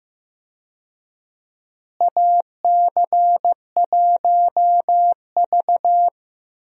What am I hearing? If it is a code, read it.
Morse code, heard as a keyed tone, about 15 wpm, 700 Hz. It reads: AC1V
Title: USA Call Signs - 4 Characters - Repeated 3x - 15wpm